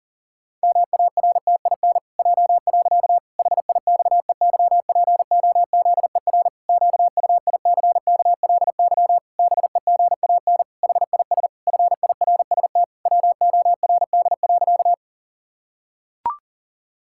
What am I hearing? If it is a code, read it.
Morse code, heard as a keyed tone, about 40 wpm, 700 Hz. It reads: MARTIN J. HIXEYPOZER QUICKLY BEGAN HIS FIRST WORD.
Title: Pangram Challenge v2 - 40wpm